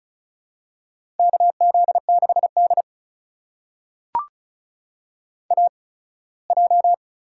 Morse code, heard as a keyed tone, about 35 wpm, spaced as for 21 wpm, 700 Hz. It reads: KZ6D AJ
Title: USA Call Signs - 4 Characters - 8x Character Spacing - 35wpm